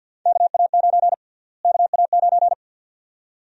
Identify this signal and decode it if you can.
Morse code, heard as a keyed tone, about 52 wpm, 700 Hz. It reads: KA9 KA9
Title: USA Call Sign Prefixes - Repeated 2x - 50wpm